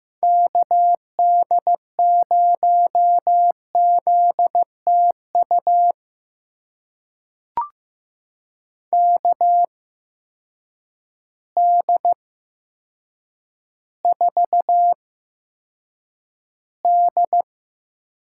Morse code, heard as a keyed tone, about 15 wpm, spaced as for 8 wpm, 700 Hz. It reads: KD0ZTU KD4D
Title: USA Call Signs - 6 Characters - 8x Character Spacing - 15wpm